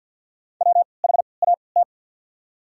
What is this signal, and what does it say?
Morse code, heard as a keyed tone, about 47 wpm, spaced as for 28 wpm, 690 Hz. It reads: WHAT
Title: 3x Character Spacing - Top 100 Words - 50wpm